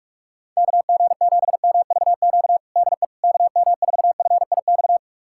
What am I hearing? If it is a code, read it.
Morse code, heard as a keyed tone, about 42 wpm, 700 Hz. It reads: KG7MVQ DE KM4FIX
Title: USA Callsign DE Callsign 45wpm